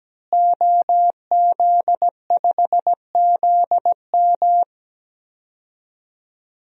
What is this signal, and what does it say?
Morse code, heard as a keyed tone, about 17 wpm, 700 Hz. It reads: OZ5ZM
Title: DXCC-100 Call Signs - 5 Characters - Repeated 3x - 17wpm